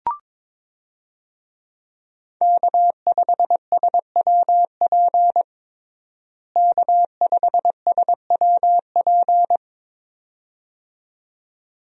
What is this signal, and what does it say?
Morse code, heard as a keyed tone, about 22 wpm, 700 Hz. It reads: K5SWP K5SWP
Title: USA Call Signs - 5 Characters - Repeated 2x - 22wpm